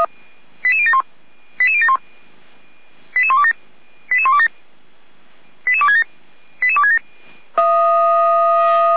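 Five-tone selective call (ZVEI), heard as quick runs of five tones, tones of about 80 ms. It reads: 88071 88071 88127 88127 88267 88267
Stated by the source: ZVEI five-tone selective call received by SDR, from feuerwehr-orlamuende.org fire brigade archive.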